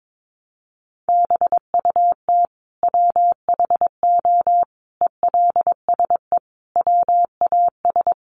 Morse code, heard as a keyed tone, about 22 wpm, 700 Hz. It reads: BUT WHO ELSE WAS
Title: Sentences from Top 800 Words 22wpm